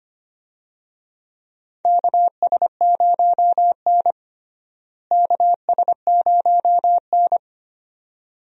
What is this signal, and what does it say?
Morse code, heard as a keyed tone, about 25 wpm, 700 Hz. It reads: KS0N KS0N
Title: USA Call Signs - 4 Characters - Repeated 2x - 25wpm